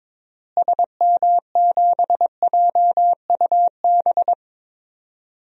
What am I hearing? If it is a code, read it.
Morse code, heard as a keyed tone, about 22 wpm, 700 Hz. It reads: SM7JUB
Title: DXCC-100 Call Signs - 6 Characters - Repeated 2x - 22wpm